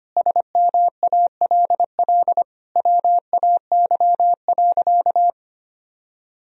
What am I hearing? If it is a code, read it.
Morse code, heard as a keyed tone, about 25 wpm, 700 Hz. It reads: SMALL WAY.
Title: Sentences from Top 1000 Words - Review Part 1 - 25wpm